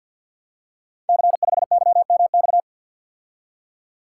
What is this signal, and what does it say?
Morse code, heard as a keyed tone, about 48 wpm, 700 Hz. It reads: C5YMX
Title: Code Groups - 5 Letter:Numbers - 50wpm